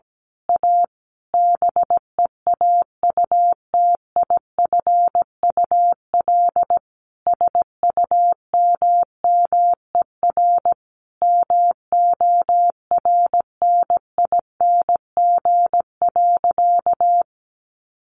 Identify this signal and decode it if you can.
Morse code, heard as a keyed tone, about 17 wpm, 700 Hz. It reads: A BEAUTIFUL SUMMER MORNING.
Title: Sentences from Top 1100 Words 17wpm